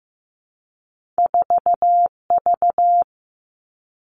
Morse code, heard as a keyed tone, about 15 wpm, 700 Hz. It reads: 4V
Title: All Cut Numbers - 2-Digit 15wpm